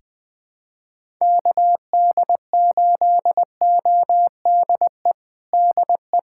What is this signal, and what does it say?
Morse code, heard as a keyed tone, about 20 wpm, 700 Hz. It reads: KD8ODE DE
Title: USA Callsign DE Callsign - 6 Characters - 20wpm